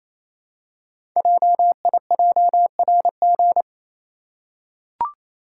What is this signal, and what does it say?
Morse code, heard as a keyed tone, about 27 wpm, 700 Hz. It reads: JIJRG
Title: Code Groups - 5 Letters - 28wpm